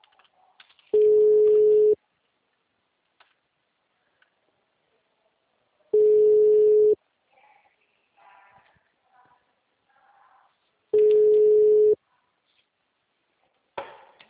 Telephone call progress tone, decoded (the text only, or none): ringback tone